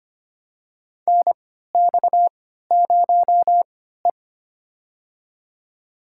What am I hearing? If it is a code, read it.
Morse code, heard as a keyed tone, about 25 wpm, spaced as for 14 wpm, 700 Hz. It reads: NX0E
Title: USA Call Signs - 4 Characters - 3x Character Spacing - 25wpm